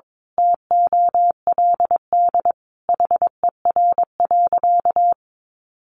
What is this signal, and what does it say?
Morse code, heard as a keyed tone, about 22 wpm, 700 Hz. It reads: TOLD HER.
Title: Sentences from Top 1200 Words 22wpm